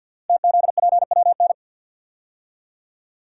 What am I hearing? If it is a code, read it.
Morse code, heard as a keyed tone, about 53 wpm, 700 Hz. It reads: TZPWN
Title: Code Groups - 5 Letter:Numbers - 50wpm